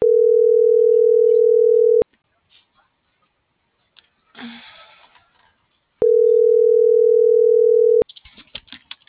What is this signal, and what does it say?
Ringback tone (North America), 440 + 480 Hz.